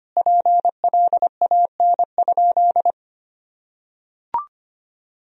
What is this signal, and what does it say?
Morse code, heard as a keyed tone, about 25 wpm, 700 Hz. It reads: PLAN?
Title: Sentences from Top 1000 Words - Review Part 3 - 25wpm